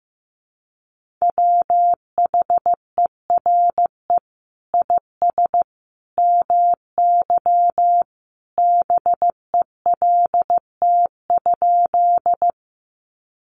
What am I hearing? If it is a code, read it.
Morse code, heard as a keyed tone, about 15 wpm, 700 Hz. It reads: WHERE IS MY BELT?
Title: Sentences from Top 1400 Words 15wpm